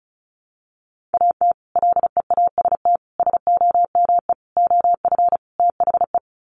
Morse code, heard as a keyed tone, about 34 wpm, 700 Hz. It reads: AT LEAST SOME OF THE
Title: Sets of 5 Words Encore 35wpm